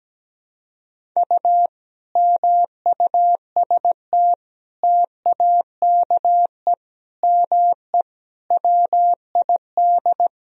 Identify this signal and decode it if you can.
Morse code, heard as a keyed tone, about 17 wpm, 700 Hz. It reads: U MUST TAKE ME WID
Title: Sentences from Top 200 Words and Abbreviations 17wpm